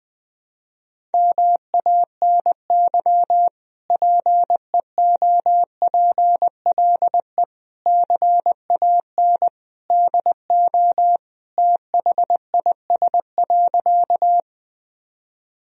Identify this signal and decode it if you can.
Morse code, heard as a keyed tone, about 20 wpm, 700 Hz. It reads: MANY PEOPLE CAN DO THIS.
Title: Sentences from Top 1000 Words - Review Part 1 - 20wpm